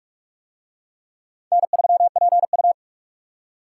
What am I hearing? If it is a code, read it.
Morse code, heard as a keyed tone, about 40 wpm, 700 Hz. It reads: N3PU